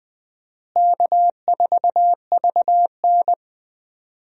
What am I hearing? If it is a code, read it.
Morse code, heard as a keyed tone, about 20 wpm, 700 Hz. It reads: K4VN
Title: USA Call Signs - 4 Characters - Repeated 2x - 20wpm